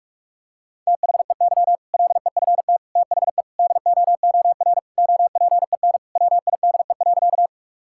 Morse code, heard as a keyed tone, about 46 wpm, 700 Hz. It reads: THEY LEFT THE DOOR OPEN WIDE.
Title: Sentences from Top 1000 Words - Review Part 6 - 45wpm